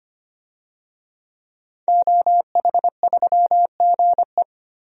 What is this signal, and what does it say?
Morse code, heard as a keyed tone, about 25 wpm, 700 Hz. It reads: OH3GE
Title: DXCC-100 Call Signs - 5 Characters - 25wpm